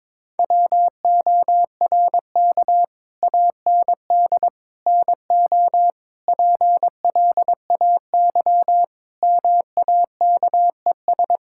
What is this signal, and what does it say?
Morse code, heard as a keyed tone, about 22 wpm, 700 Hz. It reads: WORK AND NO PLAY MAKES
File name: practice / 22wpm / Sets of 7 Words - 22wpm